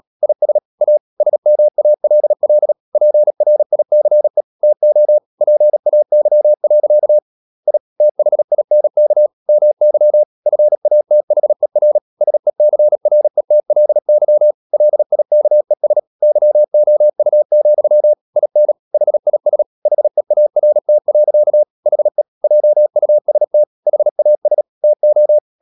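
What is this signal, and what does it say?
Morse code, heard as a keyed tone, about 37 wpm, 600 Hz. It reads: IS A SMALL PRICE TO PAY. I THINK MY FATHER SECRETLY LIKES YOU, IN HIS HEART. HE JUST HAS TO
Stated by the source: Morse practice at 36 wpm, text of Ken Follett's novel Never, from CW med Gnister.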